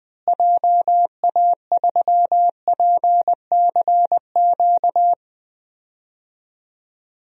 Morse code, heard as a keyed tone, about 21 wpm, 700 Hz. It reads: JA3PCQ
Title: DXCC-100 Call Signs - 6 Characters - Repeated 3x - 20wpm